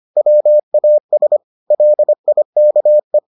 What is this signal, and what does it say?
Morse code, heard as a keyed tone, about 25 wpm, 600 Hz. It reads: WAS LIKE